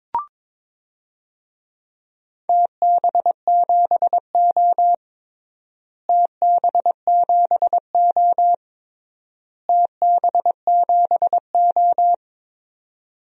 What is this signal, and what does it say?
Morse code, heard as a keyed tone, about 22 wpm, 700 Hz. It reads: TB7O TB7O TB7O
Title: DXCC-100 Call Signs - 4 Characters - Repeated 3x - 22wpm